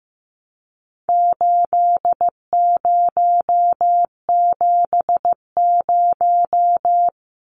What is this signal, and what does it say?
Morse code, heard as a keyed tone, about 15 wpm, 700 Hz. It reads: 8070
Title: Numbers - 4-Digit 15wpm